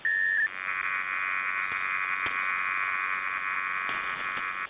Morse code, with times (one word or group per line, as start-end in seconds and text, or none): none